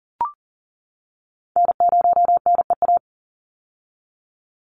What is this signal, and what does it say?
Morse code, heard as a keyed tone, about 40 wpm, 700 Hz. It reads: N0NEA
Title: USA Call Signs 40wpm